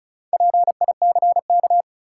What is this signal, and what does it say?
Morse code, heard as a keyed tone, about 36 wpm, 700 Hz. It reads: PICK
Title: Sentences from Top 1000 Words - New Word Drill - 35wpm